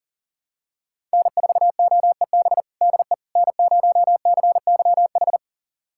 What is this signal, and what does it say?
Morse code, heard as a keyed tone, about 40 wpm, 700 Hz. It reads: N4OEB DE N0CYH
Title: USA Callsign DE Callsign 40wpm